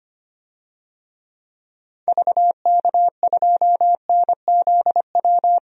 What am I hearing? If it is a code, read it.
Morse code, heard as a keyed tone, about 25 wpm, 700 Hz. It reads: VK2NZW
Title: DXCC-100 Call Signs - 6 Characters - 25wpm